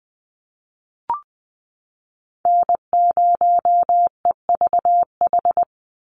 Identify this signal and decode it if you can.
Morse code, heard as a keyed tone, about 20 wpm, 700 Hz. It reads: N0EVH
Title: NA SOTA Chaser Callsigns 20wpm